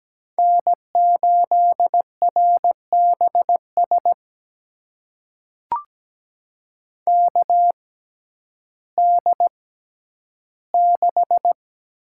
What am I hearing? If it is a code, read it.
Morse code, heard as a keyed tone, about 17 wpm, spaced as for 10 wpm, 700 Hz. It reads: N8RBS KD6
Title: USA Call Signs - 6x Character Spacing - 17wpm